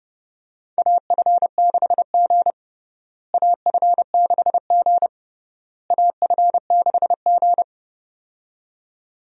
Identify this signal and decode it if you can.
Morse code, heard as a keyed tone, about 30 wpm, 700 Hz. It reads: AF6G AF6G AF6G
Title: USA Call Signs - 4 Characters - Repeated 3x - 30wpm